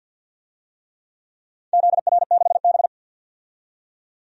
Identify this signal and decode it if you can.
Morse code, heard as a keyed tone, about 51 wpm, 700 Hz. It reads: ZR6B
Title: DXCC-100 Call Signs - 4 Characters - 50wpm